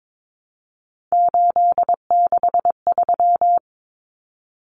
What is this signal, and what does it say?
Morse code, heard as a keyed tone, about 22 wpm, 700 Hz. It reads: 863
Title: Common Cut Numbers - 3-Digit 22wpm